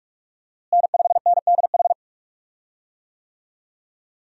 Morse code, heard as a keyed tone, about 46 wpm, 700 Hz. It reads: N5NDH
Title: USA Call Signs - 5 Characters - Repeated 2x - 45wpm